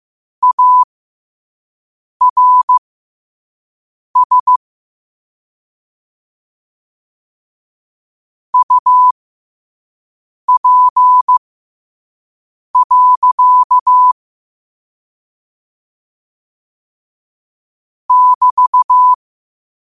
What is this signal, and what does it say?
Morse code, heard as a keyed tone, about 15 wpm, spaced as for 5 wpm, 1000 Hz. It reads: ARS UP. <BT>
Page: New quotes every day in morse code at 5 Words per minute.